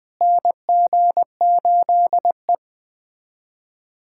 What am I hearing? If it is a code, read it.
Morse code, heard as a keyed tone, about 20 wpm, 700 Hz. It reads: NG8E
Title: USA Callsign DE Callsign - 4 Characters - 20wpm